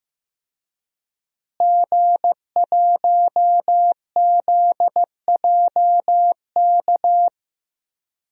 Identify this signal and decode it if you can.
Morse code, heard as a keyed tone, about 15 wpm, 700 Hz. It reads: G1ZJK
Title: DXCC-100 Call Signs - 5 Characters - Repeated 3x - 15wpm